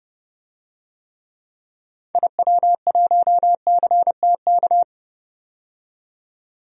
Morse code, heard as a keyed tone, about 30 wpm, 700 Hz. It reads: IW1CTK
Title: DXCC-100 Call Signs - 6 Characters - 30wpm